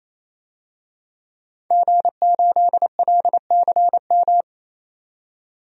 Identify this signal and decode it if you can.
Morse code, heard as a keyed tone, about 28 wpm, 700 Hz. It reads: G8LCM
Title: DXCC-100 Call Signs - 5 Characters - 28wpm